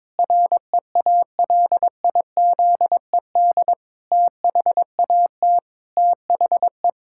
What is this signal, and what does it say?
Morse code, heard as a keyed tone, about 22 wpm, 700 Hz. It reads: REALIZED THAT THE
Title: Pangram Challenge v2 - 22wpm